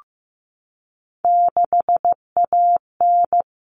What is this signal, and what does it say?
Morse code, heard as a keyed tone, about 15 wpm, 700 Hz. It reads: 6AN
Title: Common Cut Numbers - 3-Digit 15wpm